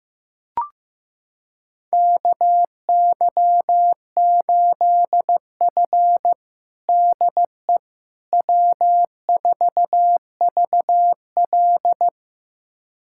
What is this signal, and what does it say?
Morse code, heard as a keyed tone, about 15 wpm, 700 Hz. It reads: KY8F DE W4VL
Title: USA Callsign DE Callsign - 4 Characters - 15wpm